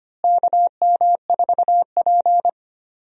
Morse code, heard as a keyed tone, about 25 wpm, 700 Hz. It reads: KM4P
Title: USA Call Signs - 4 Characters - Repeated 3x - 25wpm